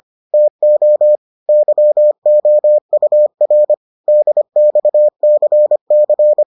Morse code, heard as a keyed tone, about 25 wpm, 600 Hz. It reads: TO YOUR DXCC